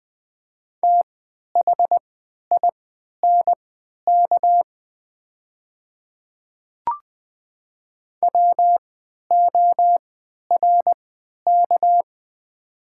Morse code, heard as a keyed tone, about 20 wpm, spaced as for 11 wpm, 700 Hz. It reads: THINK WORK
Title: 3x Character Spacing - Top 200 Words - 20wpm